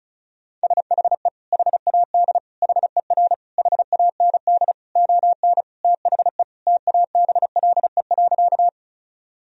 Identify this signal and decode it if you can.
Morse code, heard as a keyed tone, about 36 wpm, 700 Hz. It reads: SHE HAD HER HAND ON THE TABLE.
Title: Sentences from Top 1000 Words - Review Part 2 - 35wpm